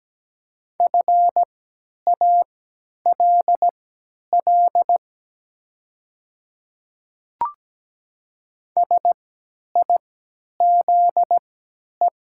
Morse code, heard as a keyed tone, about 17 wpm, spaced as for 10 wpm, 700 Hz. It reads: FALL SIZE
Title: 3x Character Spacing - Top 500 Words - 17wpm